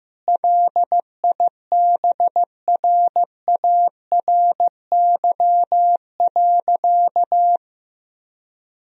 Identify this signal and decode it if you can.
Morse code, heard as a keyed tone, about 15 wpm, 700 Hz. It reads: LIBRARY.
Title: Sentences from Top 1000 Words - Review Part 4 - 15wpm